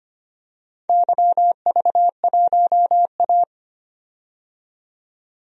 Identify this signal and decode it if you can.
Morse code, heard as a keyed tone, about 25 wpm, 700 Hz. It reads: YV1A
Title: DXCC-100 Call Signs - 4 Characters - 25wpm